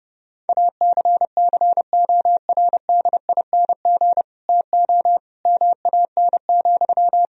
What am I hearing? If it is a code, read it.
Morse code, heard as a keyed tone, about 32 wpm, 700 Hz. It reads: ACCORDING TO MAN,
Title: Sentences from Top 1000 Words - Review Part 4 - 30wpm